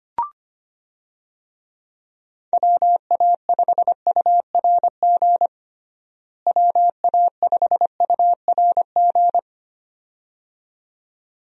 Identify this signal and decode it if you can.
Morse code, heard as a keyed tone, about 25 wpm, 700 Hz. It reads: WA5URG WA5URG
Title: USA Call Signs - 6 Characters - Repeated 2x - 25wpm